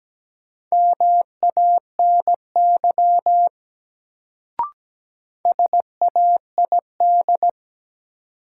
Morse code, heard as a keyed tone, about 17 wpm, 700 Hz. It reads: MANY SAID
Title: Sentences from Top 100 Words - New Word Drill - 17wpm